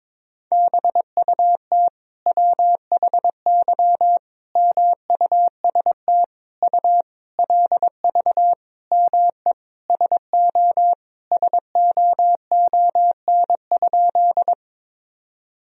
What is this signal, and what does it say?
Morse code, heard as a keyed tone, about 22 wpm, 700 Hz. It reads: BUT WHY MUST U LV ME SO SOON?
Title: Sentences from Top 400 Words and Abbreviations 22wpm